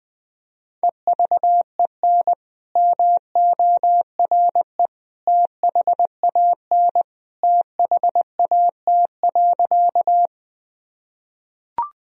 Morse code, heard as a keyed tone, about 20 wpm, 700 Hz. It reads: EVEN MORE THAN THAT.
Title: Sentences from Top 1000 Words - Review Part 1 - 20wpm